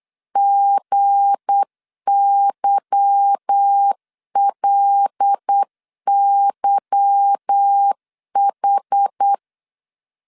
Morse code, heard as a keyed tone, about 8 wpm, 800 Hz. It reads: GYLYH
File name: LEZIONI / 8 YJ45 / 8 Y 8wpm 3245